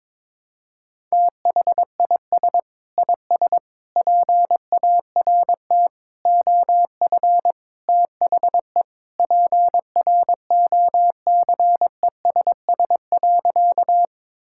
Morse code, heard as a keyed tone, about 21 wpm, 700 Hz. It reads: THIS IS PART OF THE PROCESS.
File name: practice / 22wpm / Sentences with 2-Word Phrases - 22wpm